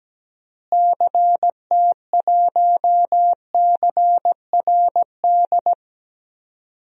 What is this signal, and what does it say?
Morse code, heard as a keyed tone, about 17 wpm, 700 Hz. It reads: CT1CRD
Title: DXCC-100 Call Signs - 6 Characters - Repeated 2x - 17wpm